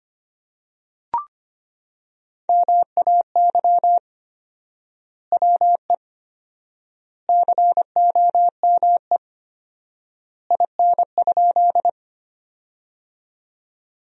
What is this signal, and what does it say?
Morse code, heard as a keyed tone, about 25 wpm, 700 Hz. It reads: MAY WE COME IN?
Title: 4x Word-Spacing - Sentences from Top 100 Words - 25wpm